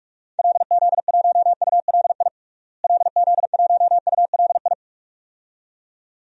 Morse code, heard as a keyed tone, about 46 wpm, 700 Hz. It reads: LZ1ULI LZ1ULI
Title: DXCC-100 Call Signs - 6 Characters - Repeated 2x - 45wpm